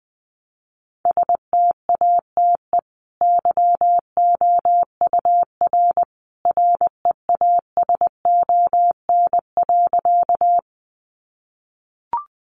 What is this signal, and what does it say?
Morse code, heard as a keyed tone, about 20 wpm, 700 Hz. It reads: STATE YOUR REASON.
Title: Sentences from Top 700 Words 20wpm